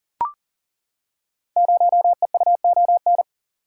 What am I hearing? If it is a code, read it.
Morse code, heard as a keyed tone, about 42 wpm, 700 Hz. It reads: 0EUON
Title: Code Groups - 5 Letter:Numbers - 40wpm